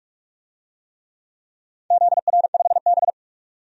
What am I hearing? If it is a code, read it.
Morse code, heard as a keyed tone, about 47 wpm, 700 Hz. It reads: ZR5B
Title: DXCC-100 Call Signs - 4 Characters - 45wpm